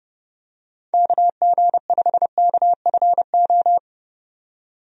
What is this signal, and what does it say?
Morse code, heard as a keyed tone, about 30 wpm, 700 Hz. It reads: KG5KFO
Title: USA Call Signs - 6 Characters - 30wpm